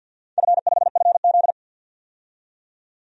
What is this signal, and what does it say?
Morse code, heard as a keyed tone, about 52 wpm, 700 Hz. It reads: F5FZ
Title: DXCC-100 Call Signs - 4 Characters - 50wpm